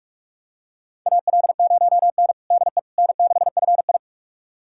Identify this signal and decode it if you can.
Morse code, heard as a keyed tone, about 46 wpm, 700 Hz. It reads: AL0N DE N6FI
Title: USA Callsign DE Callsign - 4 Characters - 45wpm